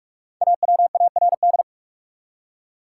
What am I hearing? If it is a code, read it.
Morse code, heard as a keyed tone, about 46 wpm, 700 Hz. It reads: AWARD
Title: Five Letter Words - 45wpm